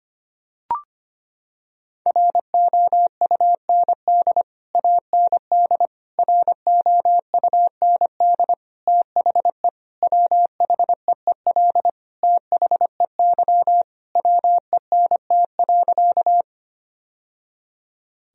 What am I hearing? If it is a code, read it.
Morse code, heard as a keyed tone, about 25 wpm, 700 Hz. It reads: ROUND AND ROUND THE WHEEL THEY WENT.
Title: Sentences from Top 1000 Words - Review Part 3 - 25wpm